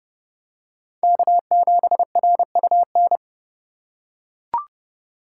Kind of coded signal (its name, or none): Morse code